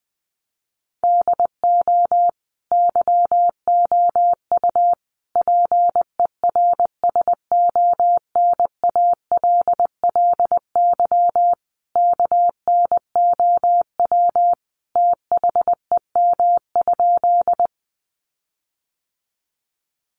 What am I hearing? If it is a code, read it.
Morse code, heard as a keyed tone, about 20 wpm, 700 Hz. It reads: DO YOU PERSONALLY KNOW THEM?
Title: Sentences from Top 1400 Words 20wpm